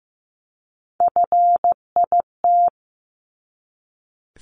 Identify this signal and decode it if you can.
Morse code, heard as a keyed tone, about 15 wpm, 700 Hz. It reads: FIT
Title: Top 751-1000 Words 15wpm